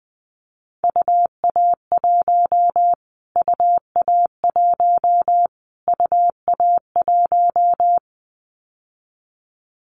Morse code, heard as a keyed tone, about 19 wpm, 700 Hz. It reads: UA1 UA1 UA1
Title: Callsign Prefixes for Top 25 DXCC - Repeated 3x - 20wpm